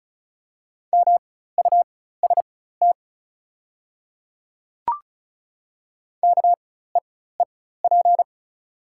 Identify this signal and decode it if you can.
Morse code, heard as a keyed tone, about 34 wpm, spaced as for 16 wpm, 700 Hz. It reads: MUST KEEP
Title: 4x Character Spacing - Top 300 Words - 35wpm